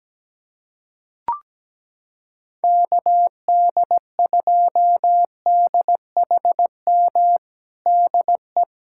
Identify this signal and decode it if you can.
Morse code, heard as a keyed tone, about 17 wpm, 700 Hz. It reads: KD2DHM DE
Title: USA Callsign DE Callsign - 6 Characters - 17wpm